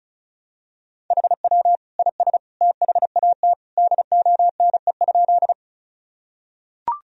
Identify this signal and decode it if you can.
Morse code, heard as a keyed tone, about 36 wpm, 700 Hz. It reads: HW IS THAT DONE?
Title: Sentences from Top 500 Words and Abbreviations 35wpm